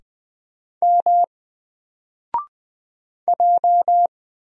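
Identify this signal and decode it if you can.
Morse code, heard as a keyed tone, about 20 wpm, 700 Hz. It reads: M J
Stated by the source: Single letter and number Morse drill at 20 wpm, from N4LCD.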